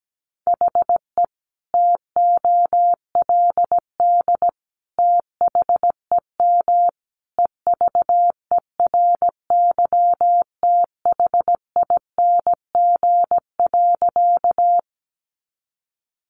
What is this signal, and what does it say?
Morse code, heard as a keyed tone, about 17 wpm, 700 Hz. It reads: HE TOLD THEM EVERYTHING.
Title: Sentences from Top 700 Words 17wpm